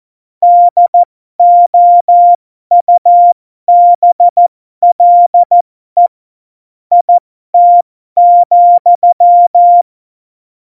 Morse code, heard as a keyed tone, about 14 wpm, spaced as for 12 wpm, 700 Hz. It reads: DOUBLE IT,
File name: Morsekurs / Witze